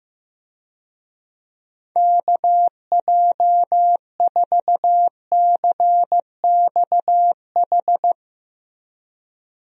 Morse code, heard as a keyed tone, about 15 wpm, 700 Hz. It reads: KJ4CXH
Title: USA Call Signs - 6 Characters - Repeated 3x - 15wpm